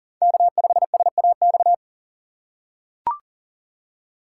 Morse code, heard as a keyed tone, about 40 wpm, 700 Hz. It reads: K5SAX
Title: USA Call Signs - 5 Characters - 2x Character Spacing - 40wpm